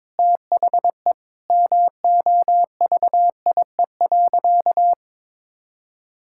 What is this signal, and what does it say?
Morse code, heard as a keyed tone, about 22 wpm, 700 Hz. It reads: THE MOVIE.
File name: practice / 22wpm / Sentences with 2-Word Phrases - 22wpm